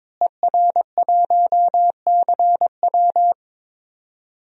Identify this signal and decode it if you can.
Morse code, heard as a keyed tone, about 22 wpm, 700 Hz. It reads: ER1CW
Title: DXCC-100 Call Signs - 5 Characters - Repeated 3x - 22wpm